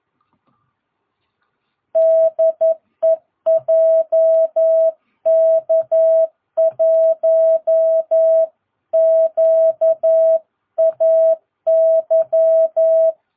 Morse code, heard as a keyed tone, about 11 wpm, 660 Hz. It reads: DE JK1QAY
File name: jh8cbh / 2022cw